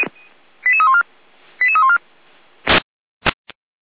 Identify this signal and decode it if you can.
Five-tone selective call (ZVEI), heard as quick runs of five tones, tones of about 70 ms. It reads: 88314 88314